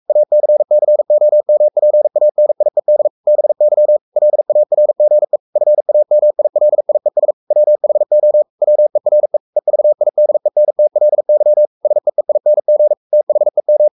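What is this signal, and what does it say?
Morse code, heard as a keyed tone, about 44 wpm, 600 Hz. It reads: ACCOMPANIED BY LARGE FAMILIES WHO WERE EVIDENTLY SEEING THEM
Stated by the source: Morse code at 43 wpm sending Ken Follett's novel Never in English, from CW med Gnister.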